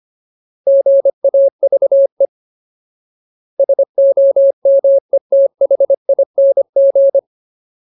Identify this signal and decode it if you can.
Morse code, heard as a keyed tone, about 25 wpm, 550 Hz. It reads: GAVE SOMETHING
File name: Ultimate Word List / 25WPM Ultimate